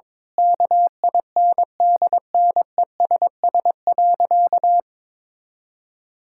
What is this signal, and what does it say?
Morse code, heard as a keyed tone, about 22 wpm, 700 Hz. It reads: KINDNESS.
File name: Difficult Sentences with Hints - 22wpm